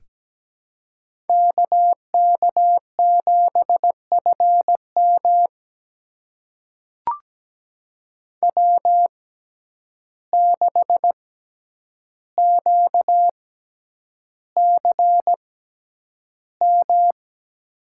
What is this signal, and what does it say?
Morse code, heard as a keyed tone, about 17 wpm, spaced as for 9 wpm, 700 Hz. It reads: KK7FM W6QCM
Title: USA Call Signs - 5 Characters - 6x Character Spacing - 17wpm